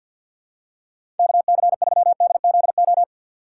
Morse code, heard as a keyed tone, about 53 wpm, 700 Hz. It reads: KC3DZO
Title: USA Call Signs - 2x Character Spacing - 50wpm